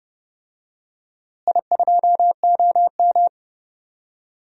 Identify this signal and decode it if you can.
Morse code, heard as a keyed tone, about 31 wpm, 700 Hz. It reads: I2OM
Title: DXCC-100 Call Signs - 4 Characters - Repeated 3x - 30wpm